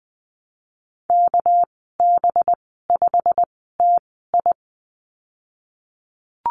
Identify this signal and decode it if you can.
Morse code, heard as a keyed tone, about 20 wpm, spaced as for 14 wpm, 700 Hz. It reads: KB5TI
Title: 213 Lesson 34g - Number 8 - Call Signs 20wpm